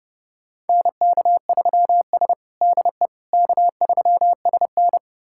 Morse code, heard as a keyed tone, about 29 wpm, 700 Hz. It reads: NK3S DE K3SN
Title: USA Callsign DE Callsign - 4 Characters - 30wpm